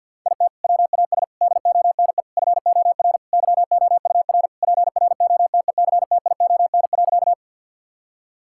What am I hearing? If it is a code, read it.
Morse code, heard as a keyed tone, about 50 wpm, 700 Hz. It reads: IT WAS DONE FOR YOUR PROTECTION.
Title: Sentences from Top 1600 Words 50wpm